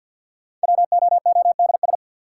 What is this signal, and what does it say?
Morse code, heard as a keyed tone, about 53 wpm, 700 Hz. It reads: WOODS
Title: Sentences from Top 1000 Words - Review Part 2 - New Word Drill - 50wpm